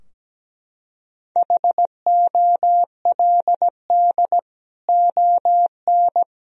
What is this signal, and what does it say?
Morse code, heard as a keyed tone, about 17 wpm, 700 Hz. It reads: HOLD ON
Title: Sentences from Top 1000 Words - Review Part 2 - 17wpm